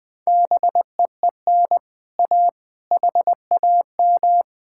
Morse code, heard as a keyed tone, about 20 wpm, 700 Hz. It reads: BEEN A HAM
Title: Speed-Racing Common QSO Phrases - 17wpm